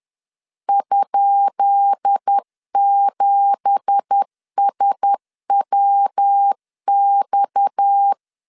Morse code, heard as a keyed tone, about 10 wpm, 800 Hz. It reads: ?7SWX
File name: LEZIONI / 10 SIGNS / 10D COMA 10wpm 3013